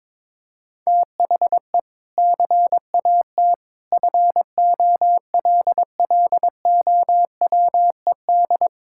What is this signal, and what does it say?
Morse code, heard as a keyed tone, about 21 wpm, 700 Hz. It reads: THE CAT FOLLOWED
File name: practice / 22wpm / Sentences with 2-Word Phrases - 22wpm